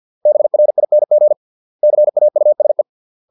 Morse code, heard as a keyed tone, about 50 wpm, 600 Hz. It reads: BRING CAUSE